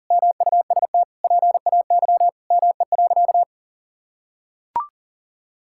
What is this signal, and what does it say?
Morse code, heard as a keyed tone, about 37 wpm, 700 Hz. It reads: MUST PAY ME.
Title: Sentences from Top 1000 Words - Review Part 5 - 40wpm